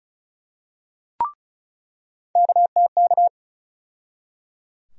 Morse code, heard as a keyed tone, about 36 wpm, 700 Hz. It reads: KTK
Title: USA Call Sign Postfixes - 3 Letters 35wpm